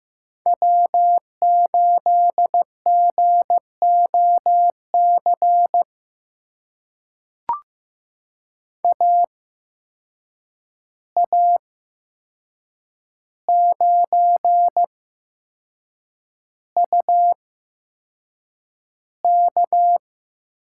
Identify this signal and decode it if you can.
Morse code, heard as a keyed tone, about 15 wpm, spaced as for 6 wpm, 700 Hz. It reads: W8GOC AA9UK
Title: USA Call Signs - 5 Characters - 8x Character Spacing - 15wpm